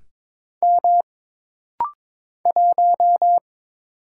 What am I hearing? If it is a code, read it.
Morse code, heard as a keyed tone, about 22 wpm, 700 Hz. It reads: M 1
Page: This rapid-fire timing is geared to push your ICR (Instant Character Recognition) skills.